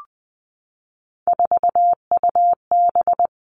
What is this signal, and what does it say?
Morse code, heard as a keyed tone, about 20 wpm, 700 Hz. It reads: 4UB
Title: All Cut Numbers - 3-Digit 20wpm